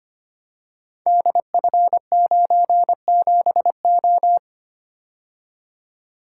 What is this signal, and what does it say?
Morse code, heard as a keyed tone, about 25 wpm, 700 Hz. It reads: DF97O
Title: Code Groups - 5 Letter:Numbers - 25wpm